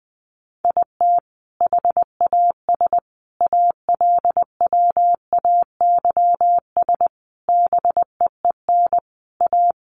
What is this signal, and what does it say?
Morse code, heard as a keyed tone, about 20 wpm, 700 Hz. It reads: IT HAS ALWAYS BEEN A
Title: Sentences from Top 1300 Words 20wpm